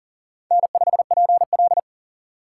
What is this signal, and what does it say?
Morse code, heard as a keyed tone, about 38 wpm, 700 Hz. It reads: N5PL
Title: USA Call Signs - 4 Characters - 6x Character Spacing - 40wpm